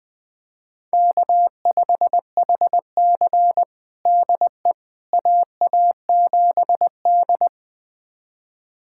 Morse code, heard as a keyed tone, about 20 wpm, 700 Hz. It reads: K5HC DE AA7D
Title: USA Callsign DE Callsign - 4 Characters - 20wpm